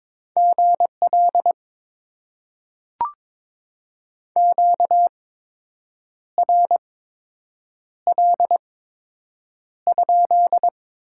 Morse code, heard as a keyed tone, about 22 wpm, spaced as for 7 wpm, 700 Hz. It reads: GL QRL?
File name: Top 90 QSO Elements - 8x Character Spacing - 22wpm